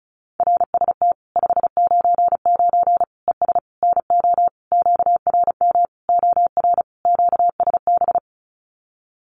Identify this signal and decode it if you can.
Morse code, heard as a keyed tone, about 34 wpm, 700 Hz. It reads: RST 599 ES NO QRM OR QSB
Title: Common QSO Phrases 35wpm